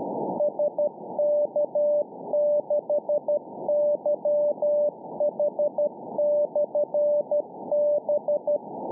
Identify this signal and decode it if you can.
Morse code, heard as a keyed tone, about 12 wpm, 590 Hz. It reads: SK6YH/B